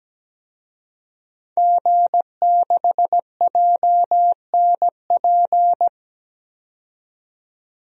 Morse code, heard as a keyed tone, about 17 wpm, 700 Hz. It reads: G6JNP
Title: DXCC-100 Call Signs - 5 Characters - 17wpm